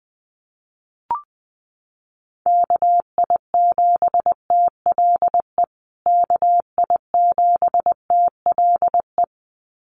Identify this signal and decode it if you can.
Morse code, heard as a keyed tone, about 20 wpm, 700 Hz. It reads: KI7TLE KI7TLE
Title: USA Call Signs Repeated 2x 20wpm